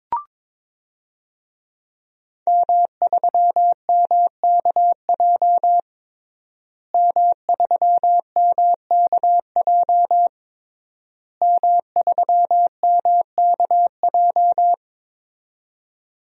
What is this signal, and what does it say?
Morse code, heard as a keyed tone, about 22 wpm, 700 Hz. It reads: M3MKJ M3MKJ M3MKJ
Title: DXCC-100 Call Signs - 5 Characters - Repeated 3x - 22wpm